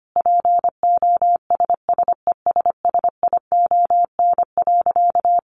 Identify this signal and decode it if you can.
Morse code, heard as a keyed tone, about 25 wpm, 700 Hz. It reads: POSSESSION.
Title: Sentences from Top 1400 Words 25wpm